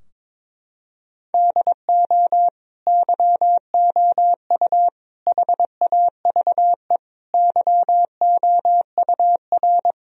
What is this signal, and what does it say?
Morse code, heard as a keyed tone, about 22 wpm, 700 Hz. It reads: DO YOU HAVE YOUR
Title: Sentences from Top 1000 Words - Review Part 3 - 22wpm